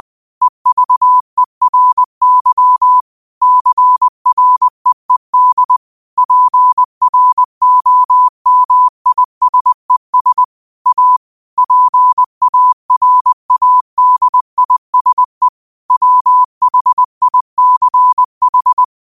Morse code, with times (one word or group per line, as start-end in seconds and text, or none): 0.4-3.0 EVERY
3.4-5.8 CREED
6.2-10.4 PROMISES
10.9-11.2 A
11.6-15.5 PARADISE
15.9-18.8 WHICH